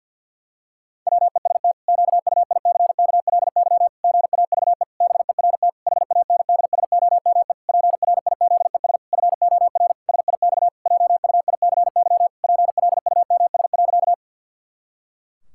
Morse code, heard as a keyed tone, about 50 wpm, 700 Hz. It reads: WEST QUICKLY GAVE BERT HANDSOME PRIZES FOR SIX JUICY PLUMS.
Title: Pangram Challenge v2 - 50wpm